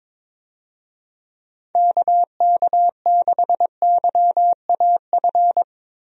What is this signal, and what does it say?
Morse code, heard as a keyed tone, about 22 wpm, 700 Hz. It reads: KK6YAF
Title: USA Call Signs - 6 Characters - Repeated 3x - 22wpm